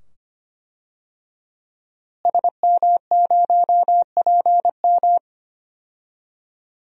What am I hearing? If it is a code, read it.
Morse code, heard as a keyed tone, about 25 wpm, 700 Hz. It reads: SM0PM
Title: DXCC-100 Call Signs - 5 Characters - Repeated 2x - 25wpm